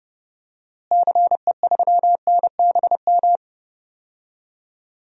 Morse code, heard as a keyed tone, about 32 wpm, 700 Hz. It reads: CE3NBM
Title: DXCC-100 Call Signs - 6 Characters - Repeated 3x - 30wpm